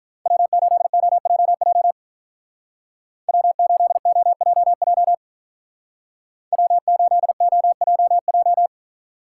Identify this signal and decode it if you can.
Morse code, heard as a keyed tone, about 43 wpm, 700 Hz. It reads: W8OJJ W8OJJ W8OJJ
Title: Speed Racing - USA 1x3 Call Signs 35wpm